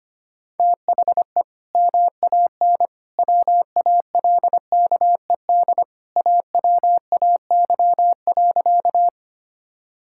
Morse code, heard as a keyed tone, about 25 wpm, 700 Hz. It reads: THE MAN WALKED AWAY.
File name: practice / 25wpm / Sentences with 2-Word Phrases - 25wpm